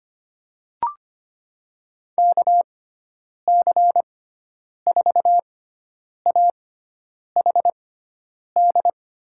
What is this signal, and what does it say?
Morse code, heard as a keyed tone, about 25 wpm, spaced as for 9 wpm, 700 Hz. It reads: KC4AHD
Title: USA Call Signs - 6x Character Spacing - 25wpm